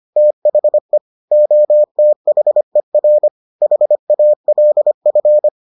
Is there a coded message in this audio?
Yes: Morse code from 0.2 to 5.5 s, heard as a keyed tone, about 25 wpm, 600 Hz. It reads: THE OTHER HALF